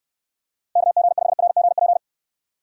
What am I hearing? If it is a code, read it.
Morse code, heard as a keyed tone, about 69 wpm, 700 Hz. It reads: KG5RGP